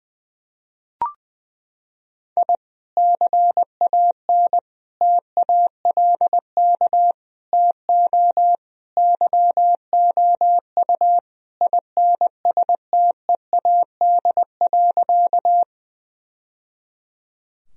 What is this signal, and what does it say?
Morse code, heard as a keyed tone, about 20 wpm, 700 Hz. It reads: I CAN TALK TO YOU INSTEAD.
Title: Sentences from Top 1000 Words - Review Part 4 - 20wpm